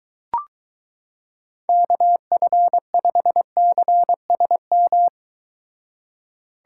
Morse code, heard as a keyed tone, about 22 wpm, 700 Hz. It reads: KF5CSM